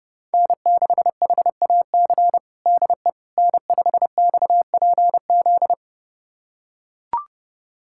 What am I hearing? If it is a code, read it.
Morse code, heard as a keyed tone, about 30 wpm, 700 Hz. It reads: N6HAC DE N5XPZ
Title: USA Callsign DE Callsign - 5 Characters - 30wpm